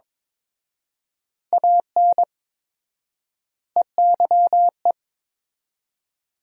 Morse code, heard as a keyed tone, about 22 wpm, 700 Hz. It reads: AN EYE
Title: 4x Word-Spacing - Sentences from Top 300 Words - 22wpm